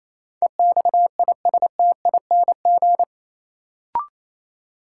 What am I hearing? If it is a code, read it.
Morse code, heard as a keyed tone, about 28 wpm, 700 Hz. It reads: EXISTING
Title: Sentences from Top 1500 Words - New Word Drill - 28wpm